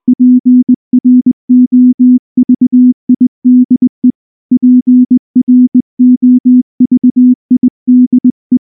Morse code, heard as a keyed tone, about 20 wpm, 250 Hz. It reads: PROVIDE PROVIDE
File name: morse / morse-training-20WPM-3